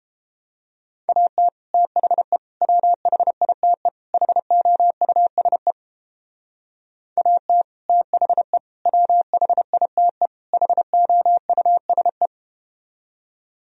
Morse code, heard as a keyed tone, about 31 wpm, 700 Hz. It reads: AT THE WHITE HOUSE AT THE WHITE HOUSE